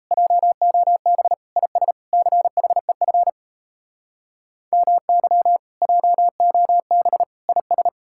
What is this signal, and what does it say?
Morse code, heard as a keyed tone, about 37 wpm, 700 Hz. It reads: JOB IS CHEF MY JOB IS